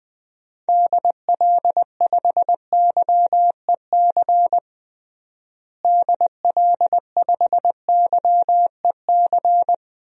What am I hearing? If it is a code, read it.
Morse code, heard as a keyed tone, about 21 wpm, 700 Hz. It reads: DL5YEC DL5YEC
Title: DXCC-100 Call Signs - 6 Characters - Repeated 2x - 20wpm